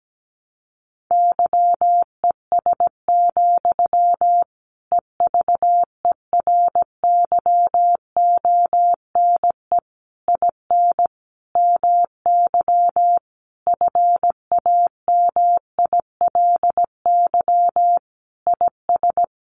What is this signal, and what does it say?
Morse code, heard as a keyed tone, about 17 wpm, 700 Hz. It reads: YES, EVERYONE IN MY FAMILY IS
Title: Sentences from Top 800 Words 17wpm